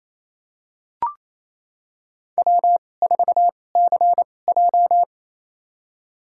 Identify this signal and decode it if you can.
Morse code, heard as a keyed tone, about 28 wpm, spaced as for 20 wpm, 700 Hz. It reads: W4CJ
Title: USA Call Signs - 4 Characters - 2x Character Spacing - 28wpm